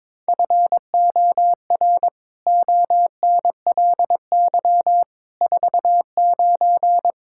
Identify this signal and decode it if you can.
Morse code, heard as a keyed tone, about 22 wpm, 700 Hz. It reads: FOR ONLY 49
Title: Pangram Challenge v2 - 22wpm